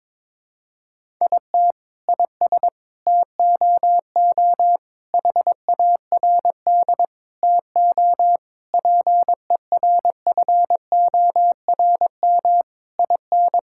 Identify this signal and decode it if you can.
Morse code, heard as a keyed tone, about 21 wpm, 700 Hz. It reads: IT IS TOO HARD TO PERFORM IN
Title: Sentences from Top 1000 Words - Review Part 5 - 22wpm